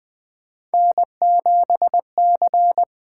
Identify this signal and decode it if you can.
Morse code, heard as a keyed tone, about 21 wpm, 700 Hz. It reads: N7C
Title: USA Call Signs - 4 Characters - 8x Character Spacing - 20wpm